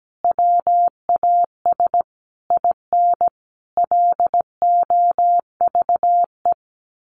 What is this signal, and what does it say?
Morse code, heard as a keyed tone, about 17 wpm, 700 Hz. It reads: WAS IN LOVE
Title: Sentences from Top 1300 Words 17wpm